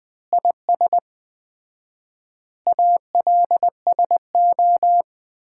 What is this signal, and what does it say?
Morse code, heard as a keyed tone, about 20 wpm, 700 Hz. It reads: IS ALSO
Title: Common QSO Phrases - 4x Word-Spacing - 20wpm